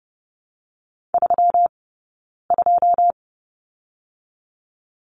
Morse code, heard as a keyed tone, about 30 wpm, spaced as for 9 wpm, 700 Hz. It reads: 32
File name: Lesson 20d - Number 3 - Two Characters 30wpm